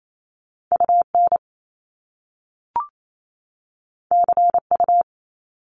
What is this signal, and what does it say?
Morse code, heard as a keyed tone, about 28 wpm, 700 Hz. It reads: UN CU
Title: Top Two Letter Combinations in Words 28wpm